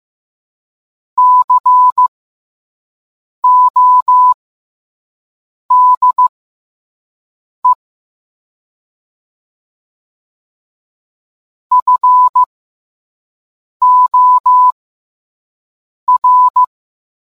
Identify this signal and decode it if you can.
Morse code, heard as a keyed tone, about 15 wpm, spaced as for 5 wpm, 1000 Hz. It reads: CODE FOR